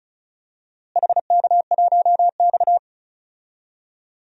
Morse code, heard as a keyed tone, about 36 wpm, 700 Hz. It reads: HK1X
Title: DXCC-100 Call Signs - 4 Characters - Repeated 2x - 35wpm